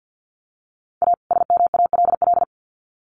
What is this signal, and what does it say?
Morse code, heard as a keyed tone, about 50 wpm, 700 Hz. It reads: A SMALL
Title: Sets of 2 Words 50wpm